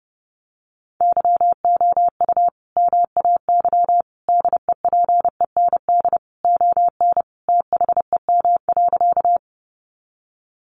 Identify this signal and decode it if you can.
Morse code, heard as a keyed tone, about 30 wpm, 700 Hz. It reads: YOU MAY DEPEND ON THEM.
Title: Sentences from Top 1300 Words 30wpm